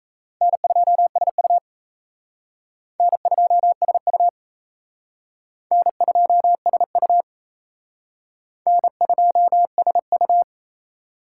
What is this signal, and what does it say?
Morse code, heard as a keyed tone, about 34 wpm, 700 Hz. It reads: N2SU N2SU N2SU N2SU
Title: Speed Racing - USA Call Signs 28wpm